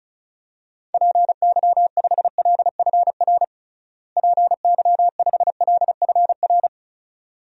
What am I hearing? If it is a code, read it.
Morse code, heard as a keyed tone, about 36 wpm, 700 Hz. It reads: PY5LFR PY5LFR
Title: DXCC-100 Call Signs - 6 Characters - Repeated 2x - 35wpm